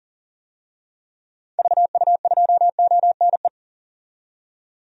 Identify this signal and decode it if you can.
Morse code, heard as a keyed tone, about 40 wpm, 700 Hz. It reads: VU2ONE
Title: DXCC-100 Call Signs - 6 Characters - 40wpm